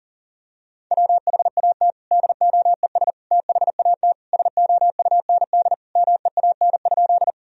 Morse code, heard as a keyed tone, about 40 wpm, 700 Hz. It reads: WHAT DOES THAT SOUND MEAN?
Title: Sentences from Top 1000 Words - Review Part 1 - 40wpm